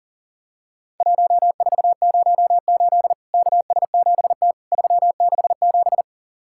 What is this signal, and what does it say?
Morse code, heard as a keyed tone, about 38 wpm, 700 Hz. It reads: 1408 KS7T 367
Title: Worked All Europe DX Contest - QTC Exchange - 40wpm